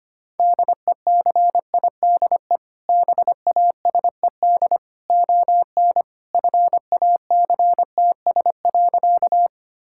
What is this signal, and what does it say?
Morse code, heard as a keyed tone, about 25 wpm, 700 Hz. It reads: DECIDE BASED ON FACTS.
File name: practice / 25wpm / Sentences with 2-Word Phrases - 25wpm